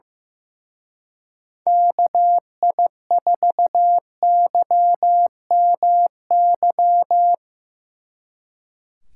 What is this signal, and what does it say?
Morse code, heard as a keyed tone, about 18 wpm, 700 Hz. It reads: KI4YMY
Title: Speed Racing - USA 2x3 Call Signs 15wpm